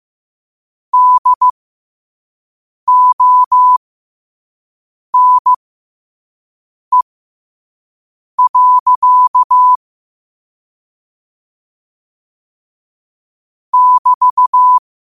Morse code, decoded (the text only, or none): DONE. <BT>